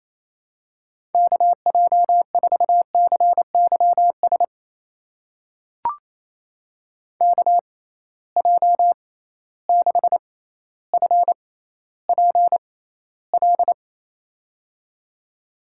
Morse code, heard as a keyed tone, about 28 wpm, spaced as for 14 wpm, 700 Hz. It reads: KJ4CYS KJ6FPL
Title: USA Call Signs - 6 Characters - 6x Character Spacing - 28wpm